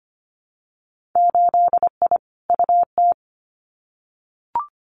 Morse code, heard as a keyed tone, about 25 wpm, 700 Hz. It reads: 8I UT
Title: Winter Field Day Exchanges Encore 25wpm